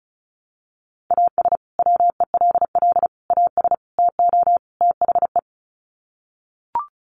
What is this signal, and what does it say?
Morse code, heard as a keyed tone, about 34 wpm, 700 Hz. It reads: AS WELL AS TO THE
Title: Sets of 5 Words Encore2 35wpm